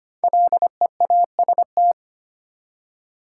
Morse code, heard as a keyed tone, about 25 wpm, 700 Hz. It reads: LEAST
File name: 5 Letter Words - First 3 Letters Given - 25wpm